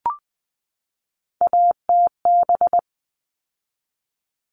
Morse code, heard as a keyed tone, about 20 wpm, 700 Hz. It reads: ATB
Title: All Cut Numbers - 3-Digit 20wpm